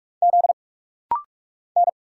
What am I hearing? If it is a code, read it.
Morse code, heard as a keyed tone, about 46 wpm, 700 Hz. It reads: Z N
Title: Single Letters - Rapid-Fire 45wpm